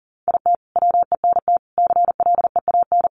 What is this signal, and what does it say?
Morse code, heard as a keyed tone, about 40 wpm, 700 Hz. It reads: IT WENT CLEAN
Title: Sentences from Top 1200 Words 40wpm